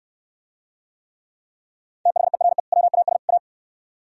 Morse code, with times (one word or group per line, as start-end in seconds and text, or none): 2.1-2.6 THERE
2.7-3.2 WAS
3.3-3.4 A